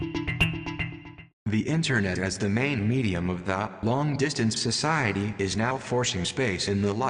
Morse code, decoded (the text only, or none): none